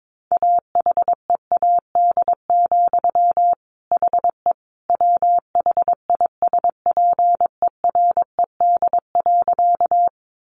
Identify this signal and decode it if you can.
Morse code, heard as a keyed tone, about 22 wpm, 700 Hz. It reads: AHEAD, HE WHISPERED.
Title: A Sound Of Thunder 22wpm